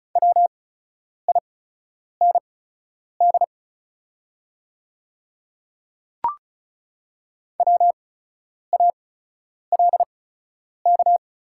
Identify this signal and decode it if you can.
Morse code, heard as a keyed tone, about 34 wpm, spaced as for 10 wpm, 690 Hz. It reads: WIND WALK
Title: 8x Character Spacing - Top 500 Words - 35wpm